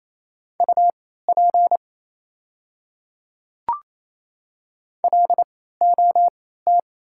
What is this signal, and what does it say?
Morse code, heard as a keyed tone, about 28 wpm, spaced as for 16 wpm, 700 Hz. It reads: UP LOT
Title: 3x Character Spacing - Top 500 Words - 28wpm